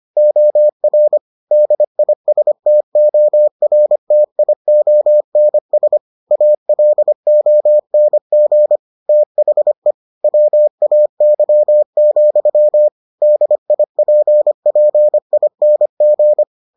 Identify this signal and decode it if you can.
Morse code, heard as a keyed tone, about 25 wpm, 600 Hz. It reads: OR DISTORTIONS ALONG THE WAY, DIPPING